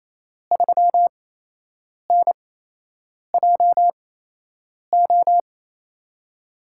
Morse code, heard as a keyed tone, about 27 wpm, spaced as for 8 wpm, 700 Hz. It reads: 3NJO
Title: USA Call Signs - 5 Characters - 8x Character Spacing - 28wpm